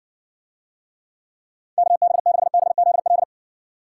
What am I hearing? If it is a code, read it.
Morse code, heard as a keyed tone, about 60 wpm, 700 Hz. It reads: KB6BZL